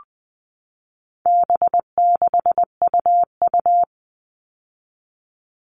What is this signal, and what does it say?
Morse code, heard as a keyed tone, about 20 wpm, 700 Hz. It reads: B6UU
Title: All Cut Numbers - 4-Digit 20wpm